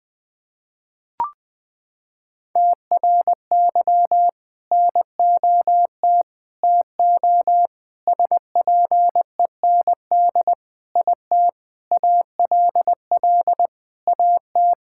Morse code, heard as a keyed tone, about 21 wpm, 700 Hz. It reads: TRY NOT TO SPEND IT ALL AT
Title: Sentences from Top 1000 Words - Review Part 5 - 20wpm